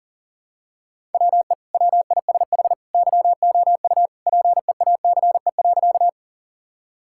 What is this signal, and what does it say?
Morse code, heard as a keyed tone, about 40 wpm, 700 Hz. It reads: WE WISH YOU PEACE.
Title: Sentences from Top 1600 Words 40wpm